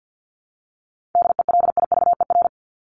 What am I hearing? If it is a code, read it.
Morse code, heard as a keyed tone, about 48 wpm, 700 Hz. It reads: DELIVER
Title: Top 1751-2000 Words 50wpm